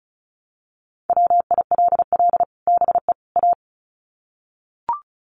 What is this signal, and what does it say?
Morse code, heard as a keyed tone, about 34 wpm, 700 Hz. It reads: WILL BE A
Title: Sets of 3 Words 35wpm